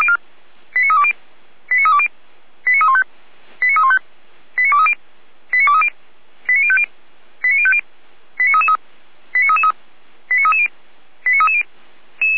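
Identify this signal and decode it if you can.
Five-tone selective call (ZVEI), heard as quick runs of five tones, tones of about 70 ms. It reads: 89320 89320 89315 89315 89230 89230 89050 89050 89333 89333 89330 89330
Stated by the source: ZVEI five-tone selective call received by SDR, logged by Feuerwehr Orlamünde.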